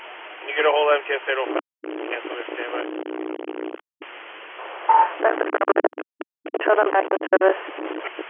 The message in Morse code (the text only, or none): none